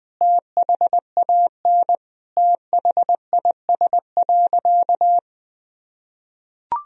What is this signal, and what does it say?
Morse code, heard as a keyed tone, about 20 wpm, 700 Hz. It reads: THAN THIS.
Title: Sentences from Top 1000 Words - Review Part 1 - 20wpm